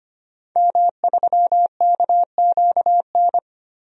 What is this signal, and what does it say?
Morse code, heard as a keyed tone, about 25 wpm, 700 Hz. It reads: M3KQN
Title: DXCC-100 Call Signs - 5 Characters - Repeated 3x - 25wpm